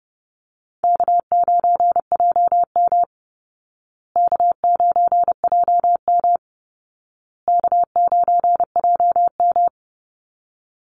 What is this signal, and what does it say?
Morse code, heard as a keyed tone, about 30 wpm, 700 Hz. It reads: K9JM K9JM K9JM
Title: Easy Call Signs Repeated 3x - 30wpm